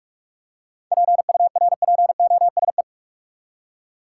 Morse code, heard as a keyed tone, about 47 wpm, 700 Hz. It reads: PURPOSE
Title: Seven Letter Words - 45wpm